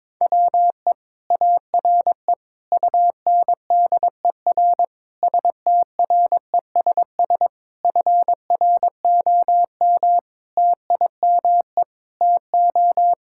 Morse code, heard as a keyed tone, about 22 wpm, 700 Hz. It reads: WE ARE UNDER STRESS FROM TIME TO
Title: Sentences from Top 1000 Words - Review Part 6 - 22wpm